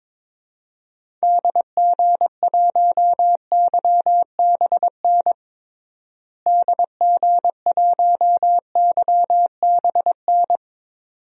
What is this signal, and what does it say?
Morse code, heard as a keyed tone, about 22 wpm, 700 Hz. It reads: DG1YBN DG1YBN
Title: DXCC-100 Call Signs - 6 Characters - Repeated 2x - 22wpm